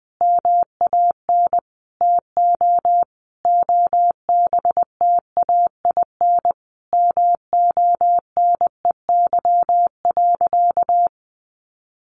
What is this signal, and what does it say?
Morse code, heard as a keyed tone, about 20 wpm, 700 Hz. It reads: MAN TO OBTAIN MONEY.
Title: Sentences from Top 1300 Words 20wpm